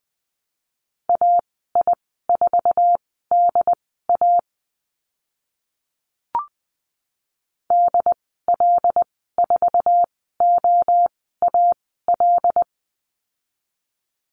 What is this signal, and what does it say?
Morse code, heard as a keyed tone, about 20 wpm, spaced as for 14 wpm, 700 Hz. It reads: AI4DA DL4OAL
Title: 122 Lesson 20g - Number 3 - Call Signs 20wpm